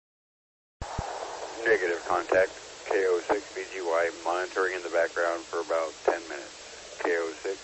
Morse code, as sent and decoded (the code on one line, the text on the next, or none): none